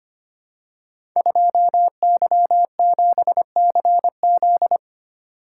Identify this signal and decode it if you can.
Morse code, heard as a keyed tone, about 25 wpm, 700 Hz. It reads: 2Y7CZ
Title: Code Groups - 5 Letter:Numbers - 25wpm